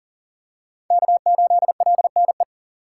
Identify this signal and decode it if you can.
Morse code, heard as a keyed tone, about 40 wpm, 700 Hz. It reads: K8LNE
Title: USA Call Signs - 2x Character Spacing - 40wpm